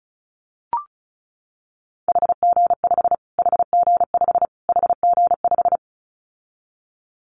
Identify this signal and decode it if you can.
Morse code, heard as a keyed tone, about 34 wpm, 700 Hz. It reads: HG5 HG5 HG5
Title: Callsign Prefixes for Top 25 DXCC - Repeated 3x - 35wpm